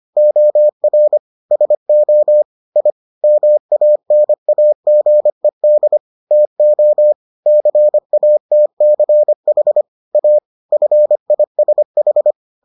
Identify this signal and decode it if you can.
Morse code, heard as a keyed tone, about 25 wpm, 600 Hz. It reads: OR SO I MANAGED TO CATCH A FISH